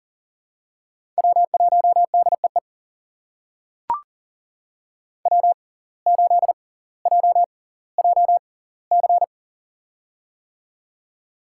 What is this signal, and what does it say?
Morse code, heard as a keyed tone, about 40 wpm, spaced as for 20 wpm, 700 Hz. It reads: W1DEE W8JJC
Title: USA Call Signs - 5 Characters - 6x Character Spacing - 40wpm